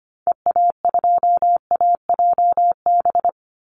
Morse code, heard as a keyed tone, about 25 wpm, 700 Hz. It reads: EA2AJB
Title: Call Signs Repeated 2x - 25wpm